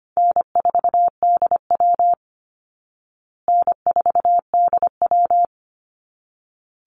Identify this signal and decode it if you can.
Morse code, heard as a keyed tone, about 25 wpm, 700 Hz. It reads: N4DW N4DW
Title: Call Signs Repeated 3x - 25wpm